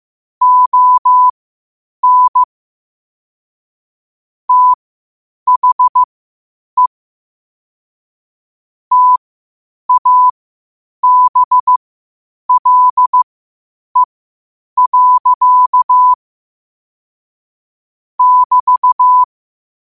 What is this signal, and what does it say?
Morse code, heard as a keyed tone, about 15 wpm, spaced as for 8 wpm, 1000 Hz. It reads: ON THE TABLE. <BT>
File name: qotdfiles / 8wpm